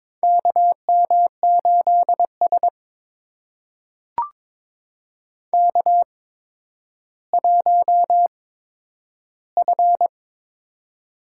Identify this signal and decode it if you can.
Morse code, heard as a keyed tone, about 22 wpm, spaced as for 11 wpm, 700 Hz. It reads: KM8S K1F
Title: USA Call Signs - 4 Characters - 8x Character Spacing - 22wpm